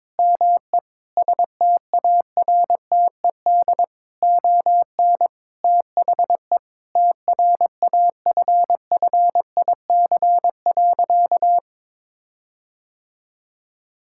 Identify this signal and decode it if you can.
Morse code, heard as a keyed tone, about 22 wpm, 700 Hz. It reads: ME STARTED ON THE TRAFFIC.
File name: practice / 22wpm / Sentences with 8-Word Phrases - 22wpm